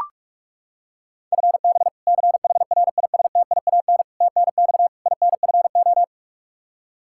Morse code, heard as a keyed tone, about 41 wpm, 700 Hz. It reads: FB CHRISTIAN TNX INFO
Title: K1USN SST QSO Practice 45wpm